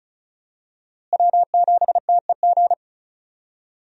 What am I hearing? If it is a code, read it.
Morse code, heard as a keyed tone, about 36 wpm, 700 Hz. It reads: W7TEG
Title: USA Call Signs - 5 Characters - Repeated 3x - 35wpm